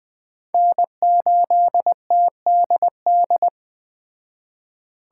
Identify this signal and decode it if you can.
Morse code, heard as a keyed tone, about 20 wpm, 700 Hz. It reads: N8TDD